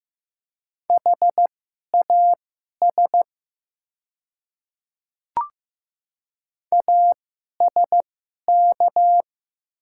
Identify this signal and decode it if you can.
Morse code, heard as a keyed tone, about 15 wpm, spaced as for 11 wpm, 700 Hz. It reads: HAS ASK
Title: 2x Character Spacing - Top 200 Words - 15wpm